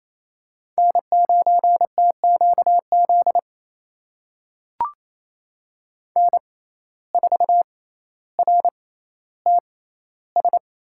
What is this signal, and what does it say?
Morse code, heard as a keyed tone, about 28 wpm, spaced as for 14 wpm, 700 Hz. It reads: N9TQZ N4RTS
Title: USA Call Signs - 5 Characters - 6x Character Spacing - 28wpm